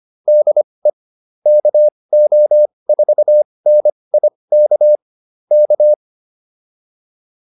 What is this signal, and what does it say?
Morse code, heard as a keyed tone, about 25 wpm, spaced as for 20 wpm, 600 Hz. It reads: DE KO4NIK K